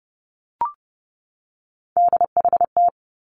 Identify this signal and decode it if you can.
Morse code, heard as a keyed tone, about 30 wpm, 700 Hz. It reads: DHT
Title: Three Challenging Letter-Number Combinations 30wpm